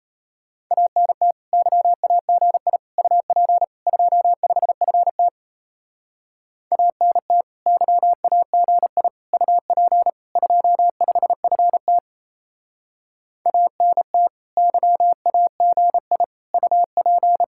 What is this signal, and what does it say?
Morse code, heard as a keyed tone, about 34 wpm, 700 Hz. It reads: ANT YAGI UP 25FT ANT YAGI UP 25FT ANT YAGI UP
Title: Speed-Racing Common QSO Phrases Encore - 28wpm